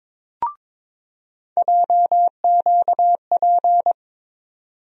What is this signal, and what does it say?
Morse code, heard as a keyed tone, about 22 wpm, 700 Hz. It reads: JQP
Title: USA Call Sign Postfixes - 3 Letters 22wpm